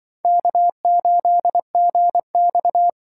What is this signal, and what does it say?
Morse code, heard as a keyed tone, about 24 wpm, 700 Hz. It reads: K8GX
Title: Speed Racing - USA 1x2 Call Signs 20wpm